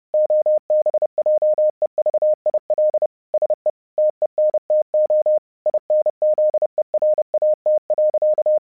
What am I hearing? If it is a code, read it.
Morse code, heard as a keyed tone, about 30 wpm, 610 Hz. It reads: OBJEVIL SE TENTO INZERAT.
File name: vyuka cw 1 / 30wpm / hasek 30wpm uvod